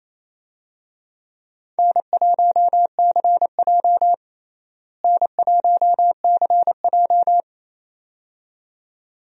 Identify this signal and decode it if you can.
Morse code, heard as a keyed tone, about 28 wpm, 700 Hz. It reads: N1CJ N1CJ
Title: USA Call Signs - 4 Characters - Repeated 2x - 28wpm